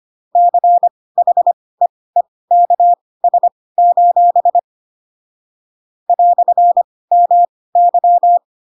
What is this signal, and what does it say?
Morse code, heard as a keyed tone, about 25 wpm, spaced as for 18 wpm, 700 Hz. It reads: CHEEKS: "MY
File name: Pinocchio / Pinocchio ch7 12 25 ik1hge